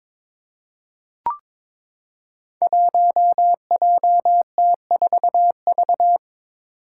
Morse code, heard as a keyed tone, about 21 wpm, 700 Hz. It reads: 1JT4V
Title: Five Challenging Letter-Number Combinations 22wpm